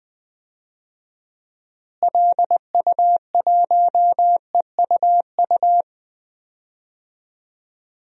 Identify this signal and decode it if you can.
Morse code, heard as a keyed tone, about 20 wpm, 700 Hz. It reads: LU1EUU
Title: DXCC-100 Call Signs - 6 Characters - Repeated 2x - 20wpm